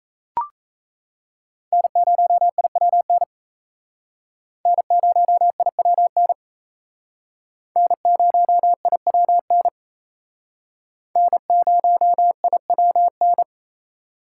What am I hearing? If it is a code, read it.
Morse code, heard as a keyed tone, about 35 wpm, 700 Hz. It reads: N0IWN N0IWN N0IWN N0IWN
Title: Speed Racing - USA Call Signs 28wpm